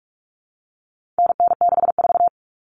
Morse code, heard as a keyed tone, about 44 wpm, 700 Hz. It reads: NN64
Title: Common Cut Numbers - 4-Digit 45wpm